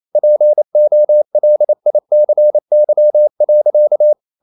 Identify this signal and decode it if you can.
Morse code, heard as a keyed tone, about 28 wpm, 600 Hz. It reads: POLICY.
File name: CW / Practice / EnglishSayings28